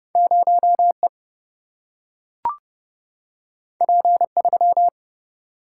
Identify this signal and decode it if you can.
Morse code, heard as a keyed tone, about 30 wpm, 700 Hz. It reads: 0 E P 3
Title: Single-Letter Single-Number 30wpm